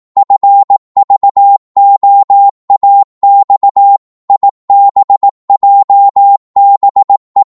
Morse code, heard as a keyed tone, about 18 wpm, 800 Hz. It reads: FVOAX IBJBE
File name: Morse / ltr-18wpm